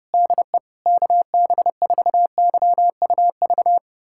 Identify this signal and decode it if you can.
Morse code, heard as a keyed tone, about 30 wpm, 700 Hz. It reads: DE KB4YUV
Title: USA Callsign DE Callsign 30wpm